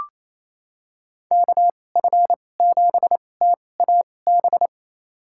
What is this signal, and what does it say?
Morse code, heard as a keyed tone, about 28 wpm, spaced as for 20 wpm, 700 Hz. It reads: KF7TAB
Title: USA Call Signs - 6 Characters - 2x Character Spacing - 28wpm